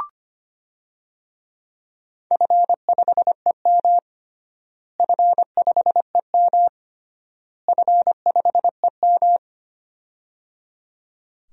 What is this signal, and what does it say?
Morse code, heard as a keyed tone, about 25 wpm, 700 Hz. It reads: F5EM F5EM F5EM
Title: DXCC-100 Call Signs - 4 Characters - Repeated 3x - 25wpm